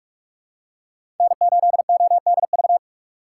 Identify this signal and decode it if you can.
Morse code, heard as a keyed tone, about 47 wpm, 700 Hz. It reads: N8ODV
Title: USA Call Signs - 5 Characters - 4x Character Spacing - 45wpm